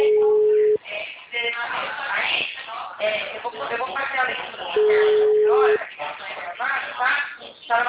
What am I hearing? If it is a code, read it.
Ringback tone (Europe, 425 Hz), with speech.
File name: out- -2001- - - .66271